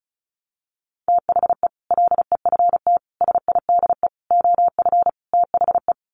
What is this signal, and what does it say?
Morse code, heard as a keyed tone, about 34 wpm, 700 Hz. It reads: THE LEFT SIDE OF THE
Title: Sets of 5 Words Encore 35wpm